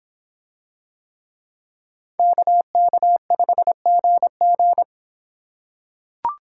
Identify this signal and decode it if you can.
Morse code, heard as a keyed tone, about 26 wpm, 700 Hz. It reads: KK5GG